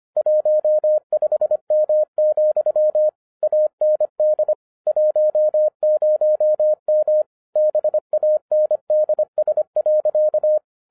Morse code, heard as a keyed tone, about 25 wpm, 610 Hz. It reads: 15M, AND 10M BANDS.